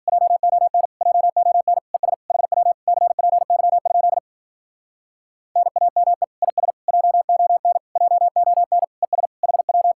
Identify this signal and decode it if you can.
Morse code, heard as a keyed tone, about 52 wpm, 700 Hz. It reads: JON JON ES HW CPY? NAME IS JON JON ES HW